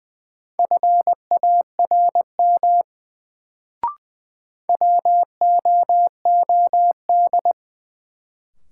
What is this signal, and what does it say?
Morse code, heard as a keyed tone, about 20 wpm, 700 Hz. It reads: FARM WOOD
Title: Sentences from Top 1000 Words - Review Part 2 - New Word Drill - 20wpm